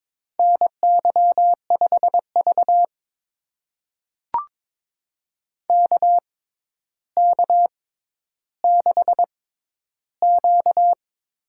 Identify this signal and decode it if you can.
Morse code, heard as a keyed tone, about 22 wpm, spaced as for 11 wpm, 700 Hz. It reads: NY5V KK6Q
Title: USA Call Signs - 4 Characters - 6x Character Spacing - 22wpm